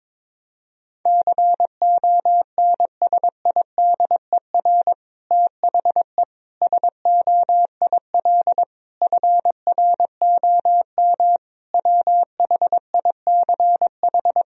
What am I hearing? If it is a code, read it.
Morse code, heard as a keyed tone, about 22 wpm, 700 Hz. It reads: CONSIDER THE SOIL FROM WHICH
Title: Sentences from Top 1000 Words - Review Part 4 - 22wpm